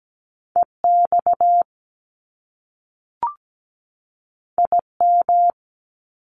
Morse code, heard as a keyed tone, about 17 wpm, 700 Hz. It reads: EX IM
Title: Top Two Letter Combinations in Words 17wpm